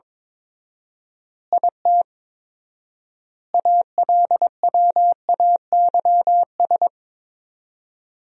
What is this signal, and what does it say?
Morse code, heard as a keyed tone, about 22 wpm, 700 Hz. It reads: IT ALWAYS
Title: 4x Word-Spacing - Sentences from Top 300 Words - 22wpm